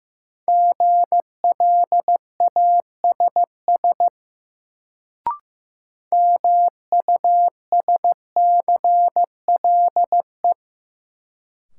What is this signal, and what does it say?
Morse code, heard as a keyed tone, about 15 wpm, 700 Hz. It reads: GLASS MUSCLE
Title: Sentences from Top 1300 Words - New Word Drill Alt - 15wpm